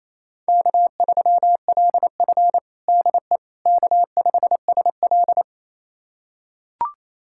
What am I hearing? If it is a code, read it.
Morse code, heard as a keyed tone, about 28 wpm, 700 Hz. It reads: K3LF DE K5SL
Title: USA Callsign DE Callsign - 4 Characters - 28wpm